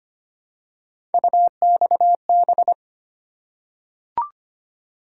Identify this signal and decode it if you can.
Morse code, heard as a keyed tone, about 25 wpm, 700 Hz. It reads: UXB
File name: Three Letter-Number Combinations - 25wpm